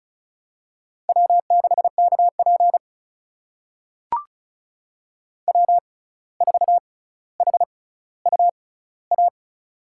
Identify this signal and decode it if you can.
Morse code, heard as a keyed tone, about 36 wpm, spaced as for 17 wpm, 700 Hz. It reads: W6KP W4HUA
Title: USA Call Signs - 6x Character Spacing - 35wpm